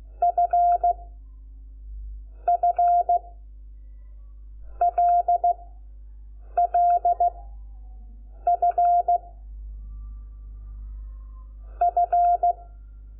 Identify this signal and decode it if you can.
Morse code, heard as a keyed tone, about 16 wpm, 700 Hz. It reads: F F L L F F
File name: Lecciones / CLASE Nº5